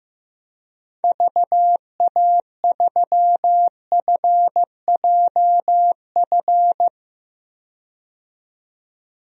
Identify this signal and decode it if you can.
Morse code, heard as a keyed tone, about 16 wpm, 700 Hz. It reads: VA3FJF
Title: DXCC-100 Call Signs - 6 Characters - 15wpm